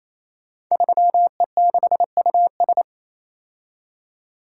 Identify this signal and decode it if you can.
Morse code, heard as a keyed tone, about 27 wpm, 700 Hz. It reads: 3E6US
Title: Code Groups - 5 Letter:Numbers - 28wpm